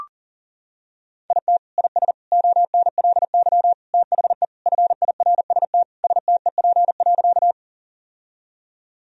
Morse code, heard as a keyed tone, about 40 wpm, 700 Hz. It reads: IT IS ONLY THE FIRST STEP.
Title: Sentences from Top 1000 Words - Review Part 2 - 40wpm